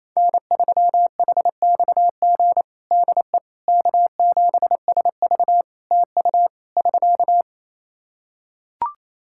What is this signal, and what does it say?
Morse code, heard as a keyed tone, about 28 wpm, 700 Hz. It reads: N3HXG DE K7SV TU <SK>
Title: K1USN SST QSO Practice 28wpm